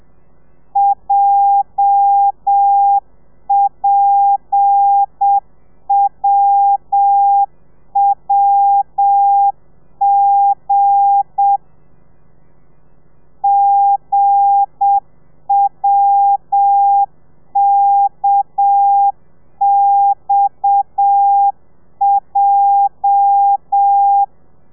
Morse code, heard as a keyed tone, about 7 wpm, 800 Hz. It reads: JPWWG GWKXJ
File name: telegrafia / i2tfj / Lez8 30